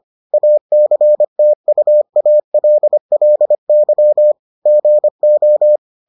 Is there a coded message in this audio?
Yes: Morse code from 0.3 to 5.8 s, heard as a keyed tone, about 25 wpm, 600 Hz. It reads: ACTUALLY GO